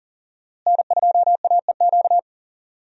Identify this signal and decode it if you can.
Morse code, heard as a keyed tone, about 40 wpm, 700 Hz. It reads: N2AEQ
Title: USA Call Signs - 5 Characters - Repeated 3x - 40wpm